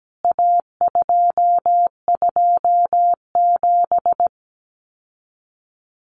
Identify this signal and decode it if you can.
Morse code, heard as a keyed tone, about 17 wpm, 700 Hz. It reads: A227
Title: CWT Exchange 17wpm